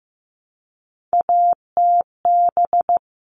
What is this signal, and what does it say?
Morse code, heard as a keyed tone, about 15 wpm, 700 Hz. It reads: ATB
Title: All Cut Numbers - 3-Digit 15wpm